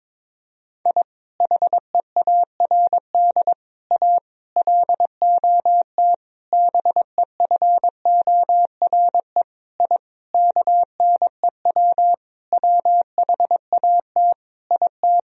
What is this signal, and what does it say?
Morse code, heard as a keyed tone, about 22 wpm, 700 Hz. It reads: I HEARD A LOT BEFORE I KNEW WHAT IT
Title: Sentences from Top 1000 Words - Review Part 3 - 22wpm